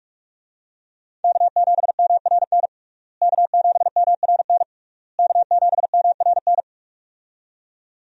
Morse code, heard as a keyed tone, about 46 wpm, 700 Hz. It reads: K7MRN K7MRN K7MRN
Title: USA Call Signs - 5 Characters - Repeated 3x - 45wpm